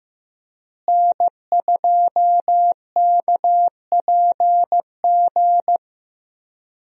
Morse code, heard as a keyed tone, about 15 wpm, 700 Hz. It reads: N2KPG
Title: USA Call Signs - 5 Characters - Repeated 2x - 15wpm